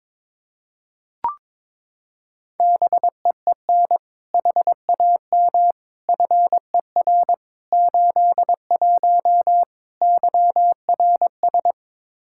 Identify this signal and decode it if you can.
Morse code, heard as a keyed tone, about 21 wpm, 700 Hz. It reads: BEEN HAM FER 81 YRS
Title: Practice QSO - Exchange 3 - More Info - 22wpm